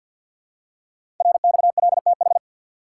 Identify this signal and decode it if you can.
Morse code, heard as a keyed tone, about 48 wpm, 700 Hz. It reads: RXLTH
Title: Code Groups - 5 Letters - 50wpm